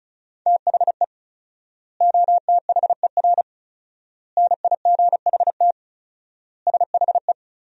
Morse code, heard as a keyed tone, about 36 wpm, 700 Hz. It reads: THE OTHER NIGHT SHE
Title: 4x Word-Spacing - Sentences from Top 300 Words - 35wpm